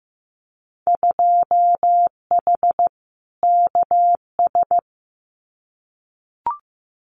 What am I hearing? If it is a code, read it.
Morse code, heard as a keyed tone, about 15 wpm, 700 Hz. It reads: 2H KS
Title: Winter Field Day Exchanges Encore 15wpm